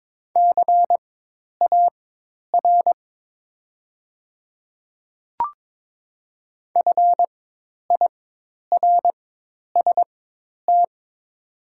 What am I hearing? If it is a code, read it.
Morse code, heard as a keyed tone, about 22 wpm, spaced as for 10 wpm, 700 Hz. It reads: CAR FIRST
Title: 4x Character Spacing - Top 300 Words - 22wpm